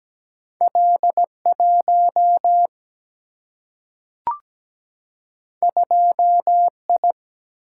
Morse code, heard as a keyed tone, about 17 wpm, 700 Hz. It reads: L 1 2 I
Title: Single-Letter Single-Number 17wpm